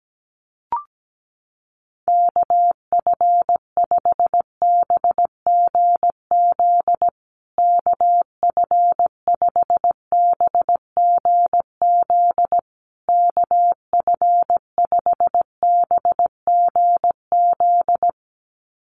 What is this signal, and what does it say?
Morse code, heard as a keyed tone, about 17 wpm, 700 Hz. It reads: KF5BGZ KF5BGZ KF5BGZ
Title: USA Call Signs Repeated 3x 17wpm